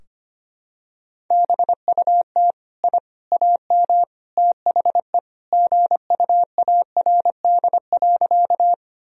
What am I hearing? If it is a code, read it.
Morse code, heard as a keyed tone, about 25 wpm, 700 Hz. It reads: BUT I AM THE GUARD.
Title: Sentences from Top 1000 Words - Review Part 4 - 25wpm